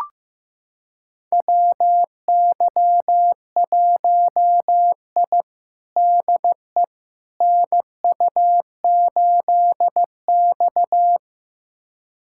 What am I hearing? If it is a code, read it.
Morse code, heard as a keyed tone, about 16 wpm, 700 Hz. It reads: WY1I DE NU8X
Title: USA Callsign DE Callsign - 4 Characters - 15wpm